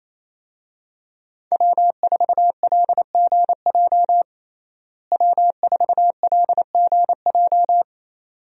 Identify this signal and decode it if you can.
Morse code, heard as a keyed tone, about 27 wpm, 700 Hz. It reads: W4LGJ W4LGJ
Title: USA Call Signs - 5 Characters - Repeated 2x - 28wpm